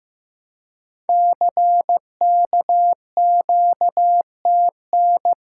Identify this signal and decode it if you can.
Morse code, heard as a keyed tone, about 15 wpm, 700 Hz. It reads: CKQTN
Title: Code Groups - 5 Letters - 15wpm